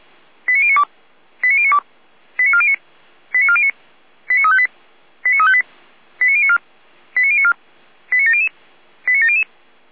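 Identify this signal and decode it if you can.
Five-tone selective call (ZVEI), heard as quick runs of five tones, tones of about 70 ms. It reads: 89092 89092 89409 89409 89358 89358 89094 89094 89700 89700
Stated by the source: ZVEI five-tone selective call received on an SDR.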